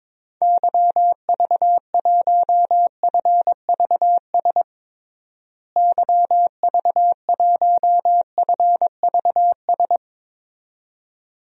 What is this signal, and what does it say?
Morse code, heard as a keyed tone, about 22 wpm, 700 Hz. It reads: YV1FVS YV1FVS
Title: DXCC-100 Call Signs - 6 Characters - Repeated 2x - 22wpm